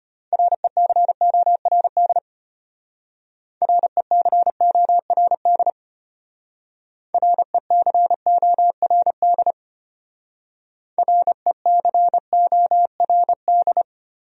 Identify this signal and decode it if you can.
Morse code, heard as a keyed tone, about 32 wpm, 700 Hz. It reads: RECORD RECORD RECORD RECORD